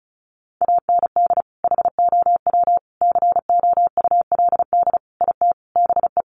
Morse code, heard as a keyed tone, about 34 wpm, 700 Hz. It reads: AND HOW COULD IT BE
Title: Sentences from Top 1200 Words 35wpm